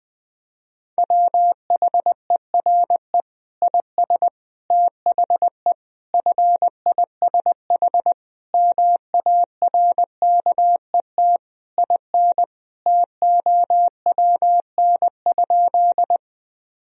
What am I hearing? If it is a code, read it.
Morse code, heard as a keyed tone, about 21 wpm, 700 Hz. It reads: WHERE IS THE FISH MARKET IN TOWN?
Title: Sentences from Top 1000 Words - Review Part 3 - 20wpm